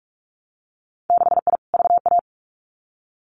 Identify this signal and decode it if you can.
Morse code, heard as a keyed tone, about 46 wpm, 700 Hz. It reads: 6I VA
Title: Winter Field Day Exchanges Encore 45wpm